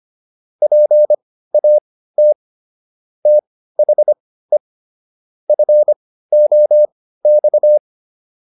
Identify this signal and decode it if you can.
Morse code, heard as a keyed tone, about 25 wpm, spaced as for 15 wpm, 600 Hz. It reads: PAT THE FOX